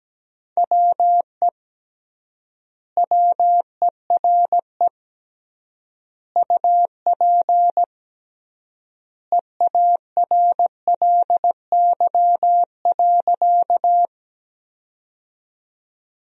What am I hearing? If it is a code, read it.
Morse code, heard as a keyed tone, about 17 wpm, 700 Hz. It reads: WE WERE UP EARLY.
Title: 3x Word-Spacing - Sentences from Top 400 Words - 17wpm